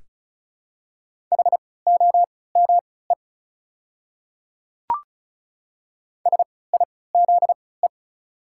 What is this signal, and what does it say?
Morse code, heard as a keyed tone, about 34 wpm, spaced as for 20 wpm, 700 Hz. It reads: HOME SIZE
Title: 3x Character Spacing - Top 500 Words - 35wpm